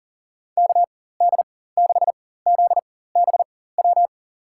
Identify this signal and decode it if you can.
Morse code, heard as a keyed tone, about 40 wpm, spaced as for 19 wpm, 700 Hz. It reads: KD6ZBW
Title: USA Call Signs - 6 Characters - 4x Character Spacing - 40wpm